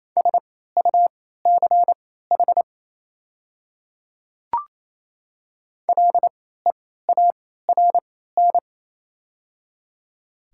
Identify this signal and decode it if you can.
Morse code, heard as a keyed tone, about 28 wpm, spaced as for 16 wpm, 700 Hz. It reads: SUCH LEARN
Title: 3x Character Spacing - Top 300 Words - 28wpm